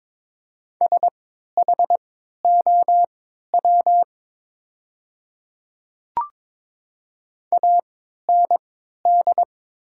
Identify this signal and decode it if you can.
Morse code, heard as a keyed tone, about 21 wpm, spaced as for 12 wpm, 700 Hz. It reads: SHOW AND
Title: 3x Character Spacing - Top 200 Words - 22wpm